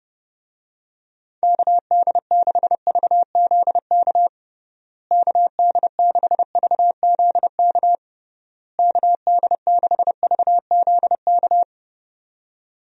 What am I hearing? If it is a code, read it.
Morse code, heard as a keyed tone, about 32 wpm, 700 Hz. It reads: KD6VZK KD6VZK KD6VZK
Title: USA Call Signs - 6 Characters - Repeated 3x - 30wpm